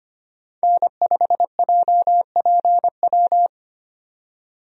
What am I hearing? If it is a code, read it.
Morse code, heard as a keyed tone, about 25 wpm, 700 Hz. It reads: N5JPW
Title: USA Call Signs - 5 Characters - 25wpm